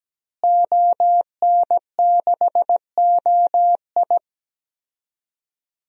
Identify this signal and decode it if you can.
Morse code, heard as a keyed tone, about 17 wpm, 700 Hz. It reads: ON6OI
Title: DXCC-100 Call Signs - 5 Characters - Repeated 2x - 17wpm